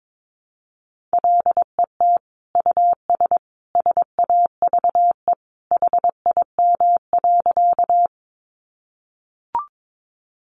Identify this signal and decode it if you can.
Morse code, heard as a keyed tone, about 22 wpm, 700 Hz. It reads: LET US SAVE HIM.
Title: Sentences from Top 1100 Words 22wpm